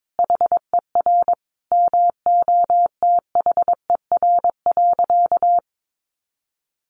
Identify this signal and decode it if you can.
Morse code, heard as a keyed tone, about 22 wpm, 700 Hz. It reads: HER MOTHER.
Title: Sentences from Top 400 Words 22wpm